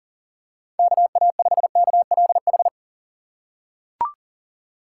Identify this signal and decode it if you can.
Morse code, heard as a keyed tone, about 40 wpm, 700 Hz. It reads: KA5KLH
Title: USA Call Signs - 4x Character Spacing - 40wpm